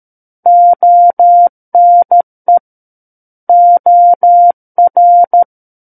Morse code, heard as a keyed tone, about 13 wpm, 700 Hz. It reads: ONE OR
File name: Book One / 04-Chapter 4 - 13 WPM